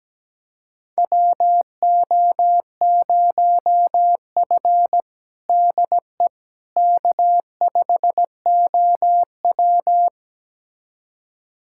Morse code, heard as a keyed tone, about 17 wpm, 700 Hz. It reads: WO0F DE K5OW
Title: USA Callsign DE Callsign - 4 Characters - 17wpm